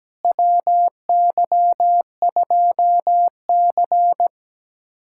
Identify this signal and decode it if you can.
Morse code, heard as a keyed tone, about 17 wpm, 700 Hz. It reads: WY2C
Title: USA Callsign DE Callsign - 4 Characters - 17wpm